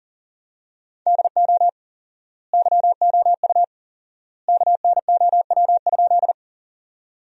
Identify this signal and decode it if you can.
Morse code, heard as a keyed tone, about 40 wpm, 700 Hz. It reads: DO YOU KNOW?
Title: 4x Word-Spacing - Sentences from Top 100 Words - 40wpm